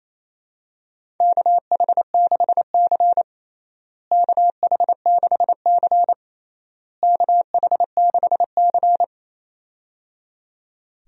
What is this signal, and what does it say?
Morse code, heard as a keyed tone, about 28 wpm, 700 Hz. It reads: KH6C KH6C KH6C
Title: DXCC-100 Call Signs - 4 Characters - Repeated 3x - 28wpm